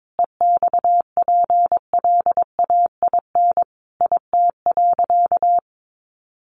Morse code, heard as a keyed tone, about 22 wpm, 700 Hz. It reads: EXPLAIN IT.
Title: Sentences from Top 1100 Words 22wpm